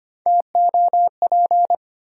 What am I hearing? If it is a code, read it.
Morse code, heard as a keyed tone, about 25 wpm, 700 Hz. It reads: TOP
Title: Sentences from Top 700 Words - New Word Drill - 25wpm